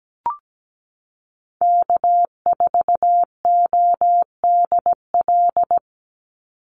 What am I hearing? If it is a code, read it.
Morse code, heard as a keyed tone, about 17 wpm, 700 Hz. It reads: K4ODL
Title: Call Signs Repeated 2x - 17wpm